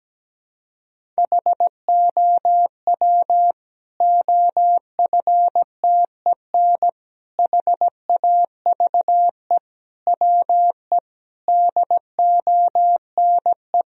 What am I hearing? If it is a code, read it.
Morse code, heard as a keyed tone, about 17 wpm, 700 Hz. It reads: HOW OFTEN HAVE WE DONE
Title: Sentences from Top 1000 Words - Review Part 1 - 17wpm